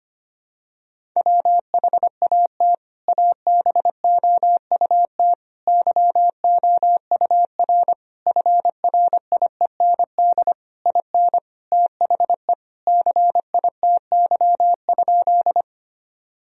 Morse code, heard as a keyed tone, about 25 wpm, 700 Hz. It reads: WHAT ABOUT YOUR FRIEND IN THE CITY?
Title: Sentences from Top 1000 Words - Review Part 1 - 25wpm